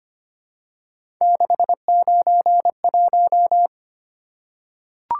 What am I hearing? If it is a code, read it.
Morse code, heard as a keyed tone, about 25 wpm, 700 Hz. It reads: 691
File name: Three Letter-Number Combinations - 25wpm